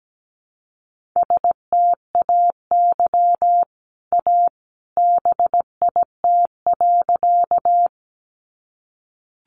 Morse code, heard as a keyed tone, about 17 wpm, 700 Hz. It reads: STAY A BIT.
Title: Sentences from Top 900 Words 17wpm